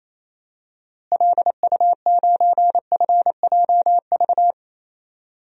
Morse code, heard as a keyed tone, about 28 wpm, 700 Hz. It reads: LU9FJV
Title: DXCC-100 Call Signs - 6 Characters - Repeated 3x - 28wpm